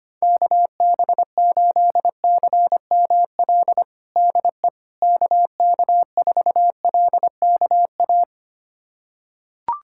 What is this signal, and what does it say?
Morse code, heard as a keyed tone, about 25 wpm, 700 Hz. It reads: KB8CML DE KK4LKA
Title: USA Callsign DE Callsign 25wpm